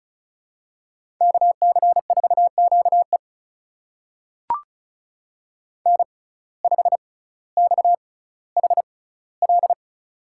Morse code, heard as a keyed tone, about 36 wpm, spaced as for 18 wpm, 700 Hz. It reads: KC4QE N5XHL
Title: USA Call Signs - 5 Characters - 6x Character Spacing - 35wpm